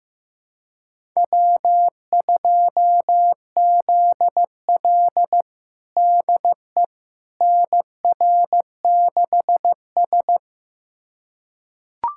Morse code, heard as a keyed tone, about 15 wpm, 700 Hz. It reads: W2ZL DE NR6S
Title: USA Callsign DE Callsign - 4 Characters - 15wpm